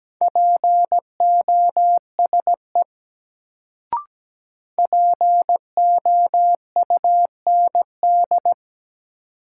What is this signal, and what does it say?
Morse code, heard as a keyed tone, about 17 wpm, 700 Hz. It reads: POSE POUND
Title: Sentences from Top 1000 Words - Review Part 2 - New Word Drill - 17wpm